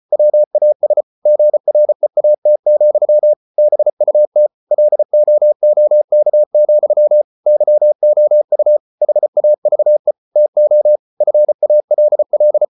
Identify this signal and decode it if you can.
Morse code, heard as a keyed tone, about 34 wpm, 600 Hz. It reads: WAS GREAT, BUT LOOK, YOU HAVE TO FALL